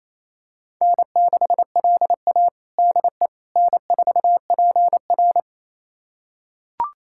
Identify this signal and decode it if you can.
Morse code, heard as a keyed tone, about 28 wpm, 700 Hz. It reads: N6LA DE N4PR
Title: USA Callsign DE Callsign - 4 Characters - 28wpm